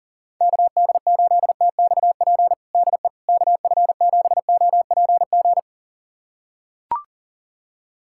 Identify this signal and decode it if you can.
Morse code, heard as a keyed tone, about 40 wpm, 700 Hz. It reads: KD8TXP DE KF7OPG
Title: USA Callsign DE Callsign - 6 Characters - 40wpm